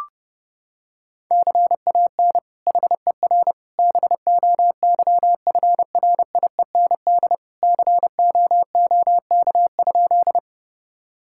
Morse code, heard as a keyed tone, about 32 wpm, 700 Hz. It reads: CAN HER BOYFRIEND COOK?
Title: Sentences from Top 1000 Words - Review Part 5 - 30wpm